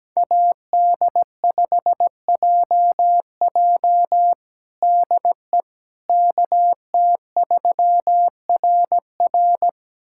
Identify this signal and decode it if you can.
Morse code, heard as a keyed tone, about 17 wpm, 700 Hz. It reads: AD5JJ DE KT3RR
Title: USA Callsign DE Callsign - 5 Characters - 17wpm